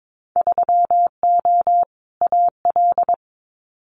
Morse code, heard as a keyed tone, about 22 wpm, 700 Hz. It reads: 3O AL
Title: Winter Field Day Exchanges 22wpm